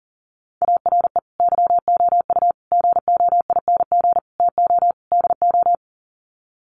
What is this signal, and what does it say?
Morse code, heard as a keyed tone, about 40 wpm, 700 Hz. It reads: ARE YOU GOING TO DO
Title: Sets of 5 Words 40wpm